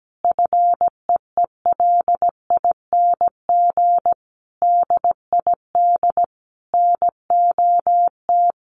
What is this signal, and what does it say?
Morse code, heard as a keyed tone, about 17 wpm, 700 Hz. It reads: FEELING DID NOT
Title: Sentences from Top 1200 Words 17wpm